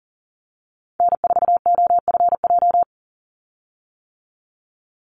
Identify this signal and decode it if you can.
Morse code, heard as a keyed tone, about 40 wpm, 700 Hz. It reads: N4OFJ
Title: USA Call Signs 40wpm